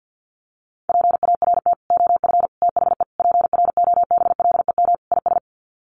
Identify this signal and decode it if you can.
Morse code, heard as a keyed tone, about 48 wpm, 700 Hz. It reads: PART OF THE PROBLEM IS
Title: Sets of 5 Words Encore2 50wpm